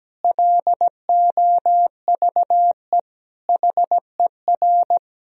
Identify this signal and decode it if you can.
Morse code, heard as a keyed tone, about 17 wpm, 700 Hz. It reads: LOVE HER
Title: Sentences from Top 1000 Words - Review Part 2 - 17wpm